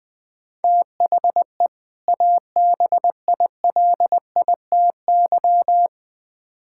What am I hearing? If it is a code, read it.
Morse code, heard as a keyed tone, about 20 wpm, 700 Hz. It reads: THE ABILITY
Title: Build A Phrase - 3-Word Phrases - 20wpm